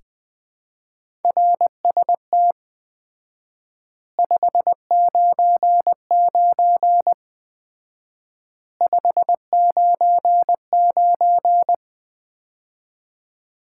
Morse code, heard as a keyed tone, about 20 wpm, 700 Hz. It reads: RST 599 599
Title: Common QSO Phrases - 4x Word-Spacing - 20wpm